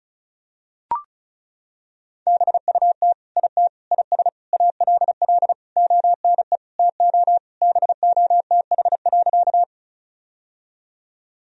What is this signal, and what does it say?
Morse code, heard as a keyed tone, about 34 wpm, 690 Hz. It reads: BUT IT IS ALL ONE TO BOTH.
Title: Sentences from Top 1000 Words - Review Part 1 - 35wpm